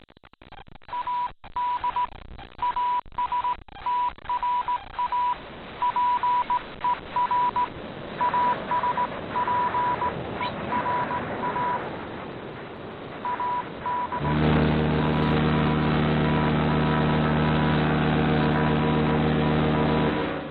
Morse code, heard as a keyed tone, about 18 wpm, 980 Hz. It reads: AD ASTRA PER ASPERA AD ASTRA PER